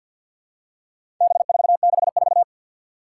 Morse code, heard as a keyed tone, about 50 wpm, 700 Hz. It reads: B464
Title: Four Challenging Letter-Number Combinations 50wpm